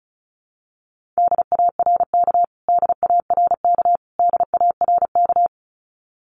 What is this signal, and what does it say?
Morse code, heard as a keyed tone, about 34 wpm, 700 Hz. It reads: DARK DARK DARK
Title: Top 500 Words Repeated 35wpm